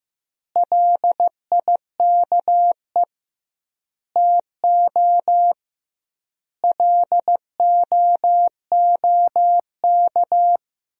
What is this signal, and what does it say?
Morse code, heard as a keyed tone, about 15 wpm, 700 Hz. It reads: LIKE TO LOOK
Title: 2x Word-Spacing - Sentences from Top 200 Words - 15wpm